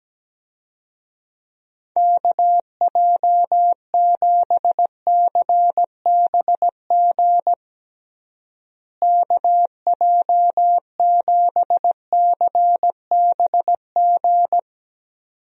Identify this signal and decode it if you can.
Morse code, heard as a keyed tone, about 17 wpm, 700 Hz. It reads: KJ7CBG KJ7CBG
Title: USA Call Signs - 6 Characters - Repeated 2x - 17wpm